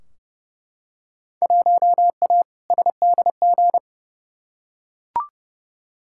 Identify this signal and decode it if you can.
Morse code, heard as a keyed tone, about 30 wpm, 700 Hz. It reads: 1A SDG
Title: Field Day Exchanges Encore - 30wpm